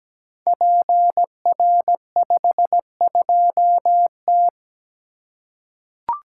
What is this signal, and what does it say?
Morse code, heard as a keyed tone, about 17 wpm, 700 Hz. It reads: PR52T
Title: Code Groups - 5 Letter:Numbers - 17wpm